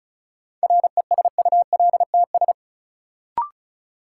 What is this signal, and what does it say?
Morse code, heard as a keyed tone, about 36 wpm, 700 Hz. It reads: RESULTS
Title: Sets of 4 Words - New Word Drill - 35wpm